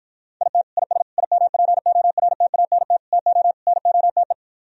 Morse code, heard as a keyed tone, about 52 wpm, 700 Hz. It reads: IT IS IMPORTANT TO NOTE